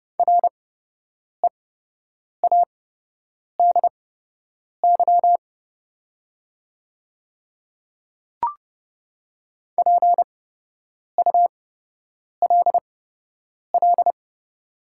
Morse code, heard as a keyed tone, about 30 wpm, spaced as for 8 wpm, 700 Hz. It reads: READY PULL
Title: 8x Character Spacing - Top 500 Words - 30wpm